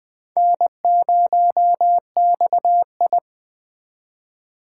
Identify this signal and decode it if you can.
Morse code, heard as a keyed tone, about 21 wpm, 700 Hz. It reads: N0XI